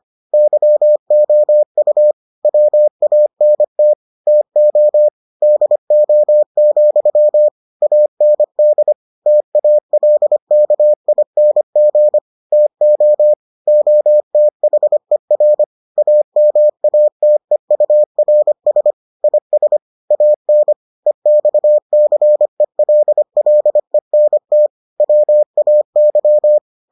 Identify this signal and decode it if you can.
Morse code, heard as a keyed tone, about 25 wpm, 600 Hz. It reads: YOU WANT TO DO, AND TALKING TO OTHER AMATEURS IS AN EXCELLENT WAY